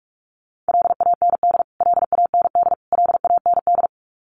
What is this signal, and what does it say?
Morse code, heard as a keyed tone, about 44 wpm, 700 Hz. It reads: LAND LAND LAND
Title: Top 500 Words Repeated 45wpm